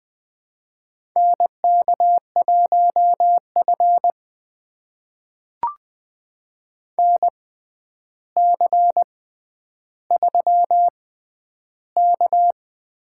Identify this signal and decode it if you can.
Morse code, heard as a keyed tone, about 20 wpm, spaced as for 10 wpm, 700 Hz. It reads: NK1F NC3K
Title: USA Call Signs - 4 Characters - 6x Character Spacing - 20wpm